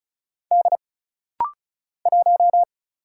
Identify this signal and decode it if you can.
Morse code, heard as a keyed tone, about 36 wpm, 700 Hz. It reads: D 1
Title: Single Letter-Number - Rapid-Fire 35wpm